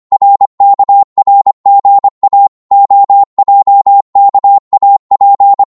Morse code, heard as a keyed tone, about 25 wpm, 800 Hz. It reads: RKRGA OJKAP
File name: Morse / ltr-25wpm